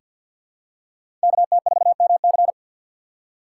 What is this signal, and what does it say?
Morse code, heard as a keyed tone, about 47 wpm, 700 Hz. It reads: KT4MC
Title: USA Call Signs - 5 Characters - 3x Character Spacing - 50wpm